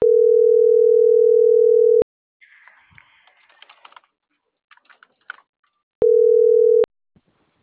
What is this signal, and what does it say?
Ringback tone (North America), 440 + 480 Hz.